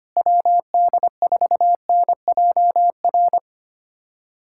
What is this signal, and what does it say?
Morse code, heard as a keyed tone, about 25 wpm, 700 Hz. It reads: WD4NJR
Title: USA Callsign DE Callsign - 6 Characters - 25wpm